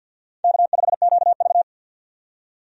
Morse code, heard as a keyed tone, about 50 wpm, 700 Hz. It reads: K5QV
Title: USA Call Signs - 4 Characters - 50wpm